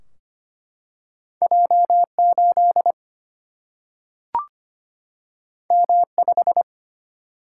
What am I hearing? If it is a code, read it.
Morse code, heard as a keyed tone, about 25 wpm, 700 Hz. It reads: J8 M5
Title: Single-Letter Single-Number 25wpm